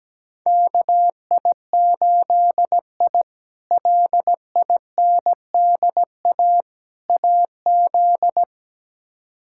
Morse code, heard as a keyed tone, about 17 wpm, 700 Hz. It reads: KI8I LINDA AZ
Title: K1USN SST QSO Practice - 17wpm